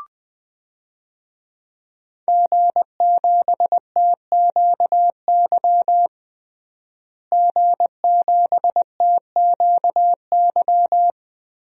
Morse code, heard as a keyed tone, about 21 wpm, 700 Hz. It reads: G7TQY G7TQY
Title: DXCC-100 Call Signs - 5 Characters - Repeated 3x - 20wpm